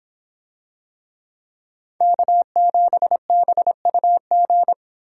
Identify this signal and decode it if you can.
Morse code, heard as a keyed tone, about 26 wpm, 700 Hz. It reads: K7BUG